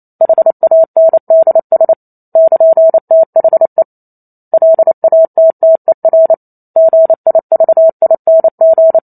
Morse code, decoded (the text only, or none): HANDS <KN>THE LATTER GIVING